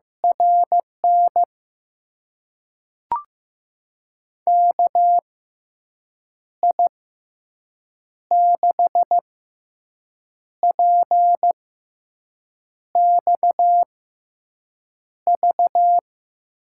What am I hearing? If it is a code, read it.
Morse code, heard as a keyed tone, about 15 wpm, spaced as for 6 wpm, 700 Hz. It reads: RN KI6PXV
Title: USA Call Signs - 6 Characters - 6x Character Spacing - 15wpm